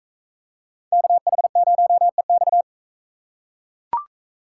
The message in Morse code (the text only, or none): KH0EX